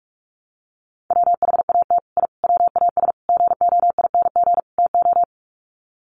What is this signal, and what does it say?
Morse code, heard as a keyed tone, about 44 wpm, 700 Hz. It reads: WHAT I WAS GOING TO
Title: Sets of 5 Words Encore2 45wpm